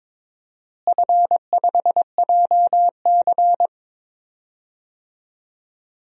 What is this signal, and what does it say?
Morse code, heard as a keyed tone, about 21 wpm, 700 Hz. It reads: F5JC
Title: DXCC-100 Call Signs - 4 Characters - Repeated 3x - 22wpm